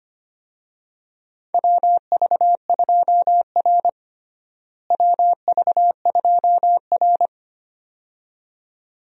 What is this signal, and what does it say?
Morse code, heard as a keyed tone, about 25 wpm, 700 Hz. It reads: WV2R WV2R
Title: USA Call Signs - 4 Characters - Repeated 2x - 25wpm